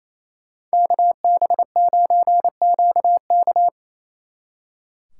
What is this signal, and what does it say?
Morse code, heard as a keyed tone, about 33 wpm, 700 Hz. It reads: KB9QK
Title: Speed Racing - USA Call Signs 28wpm